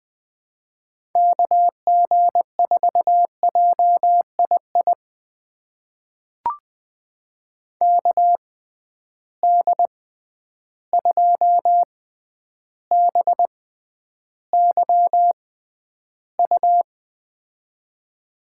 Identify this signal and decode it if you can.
Morse code, heard as a keyed tone, about 20 wpm, spaced as for 10 wpm, 700 Hz. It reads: KG4JII KD2BYU
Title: USA Call Signs - 6 Characters - 6x Character Spacing - 20wpm